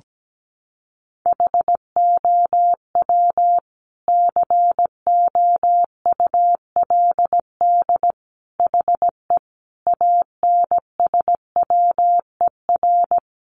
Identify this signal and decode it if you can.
Morse code, heard as a keyed tone, about 17 wpm, 700 Hz. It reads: HOW COULD HE ANSWER
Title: Sentences from Top 1300 Words 17wpm